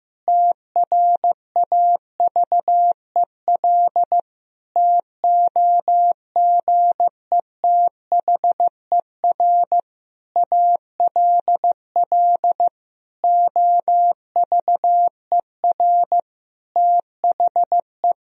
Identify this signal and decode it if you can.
Morse code, heard as a keyed tone, about 15 wpm, 700 Hz. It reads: TRAVEL TOGETHER ALL OVER THE
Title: Sentences from Top 1000 Words - Review Part 2 - 15wpm